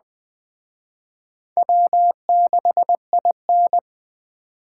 Morse code, heard as a keyed tone, about 21 wpm, 700 Hz. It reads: W6IN